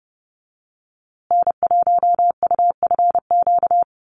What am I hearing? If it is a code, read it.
Morse code, heard as a keyed tone, about 30 wpm, 700 Hz. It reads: N1UFQ
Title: USA Call Signs Repeated 3x 30wpm